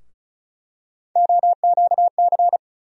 Morse code, heard as a keyed tone, about 34 wpm, 700 Hz. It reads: OQC
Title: USA Call Sign Postfixes - 3 Letters 35wpm